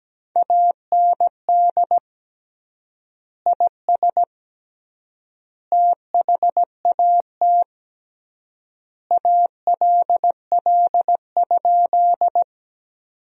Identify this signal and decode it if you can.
Morse code, heard as a keyed tone, about 17 wpm, 700 Hz. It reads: AND IS THAT ALL?
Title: 3x Word-Spacing - Sentences from Top 100 Words - 17wpm